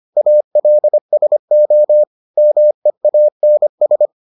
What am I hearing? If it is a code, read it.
Morse code, heard as a keyed tone, about 25 wpm, 600 Hz. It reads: ALSO MEANS